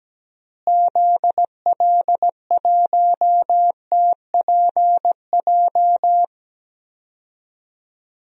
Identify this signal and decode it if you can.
Morse code, heard as a keyed tone, about 17 wpm, 700 Hz. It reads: ZL1TPJ
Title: DXCC-100 Call Signs - 6 Characters - Repeated 3x - 17wpm